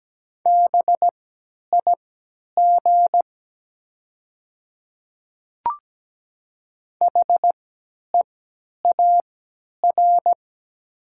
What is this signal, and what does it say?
Morse code, heard as a keyed tone, about 17 wpm, spaced as for 10 wpm, 700 Hz. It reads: BIG HEAR
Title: 3x Character Spacing - Top 300 Words - 17wpm